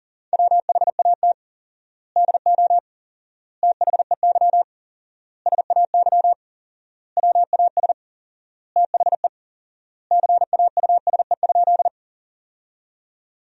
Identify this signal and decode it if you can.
Morse code, heard as a keyed tone, about 40 wpm, 700 Hz. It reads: WHAT DO THEY SAY WAS THE CAUSE?
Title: 4x Word-Spacing - Sentences from Top 200 Words - 40wpm